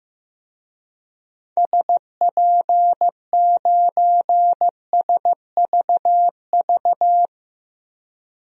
Morse code, heard as a keyed tone, about 15 wpm, 700 Hz. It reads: SP9SVV
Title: DXCC-100 Call Signs - 6 Characters - 15wpm